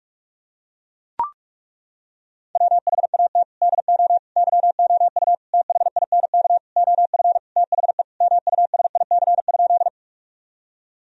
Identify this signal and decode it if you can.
Morse code, heard as a keyed tone, about 41 wpm, 700 Hz. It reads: WHAT DO YOU THINK OF THE MUSIC?
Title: Sentences from Top 1000 Words - Review Part 1 - 45wpm